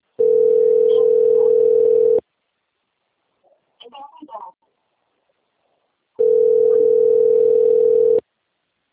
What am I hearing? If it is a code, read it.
Ringback tone (North America), 440 + 480 Hz, with speech.
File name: out- - - - - .57302